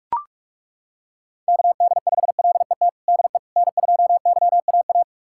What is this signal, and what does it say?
Morse code, heard as a keyed tone, about 41 wpm, 700 Hz. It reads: KD5LET DE N2YAA
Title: USA Callsign DE Callsign 45wpm